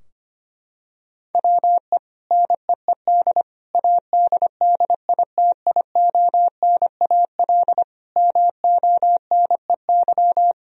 Morse code, heard as a keyed tone, about 25 wpm, 700 Hz. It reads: WE NEED ADDITIONAL MONEY
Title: Sentences from Top 1000 Words - Review Part 4 - 25wpm